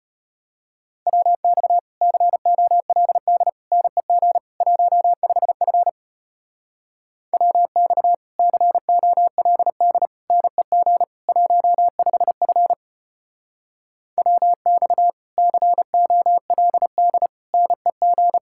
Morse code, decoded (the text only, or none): WX COLD NEG 15F WX COLD NEG 15F WX COLD NEG